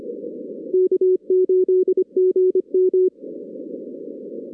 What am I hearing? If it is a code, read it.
Morse code, heard as a keyed tone, about 24 wpm, 370 Hz. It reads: K8GM
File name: 2016 arrl dx cw / 2016-ARRL-DX-CW-K5TR- - 0-22